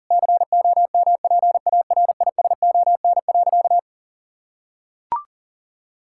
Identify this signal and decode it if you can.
Morse code, heard as a keyed tone, about 40 wpm, 700 Hz. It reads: COMPARISON.
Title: Sentences from Top 1600 Words 40wpm